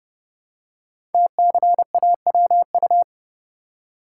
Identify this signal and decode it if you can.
Morse code, heard as a keyed tone, about 30 wpm, 700 Hz. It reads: TCAWU
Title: Code Groups - 5 Letter:Numbers - 30wpm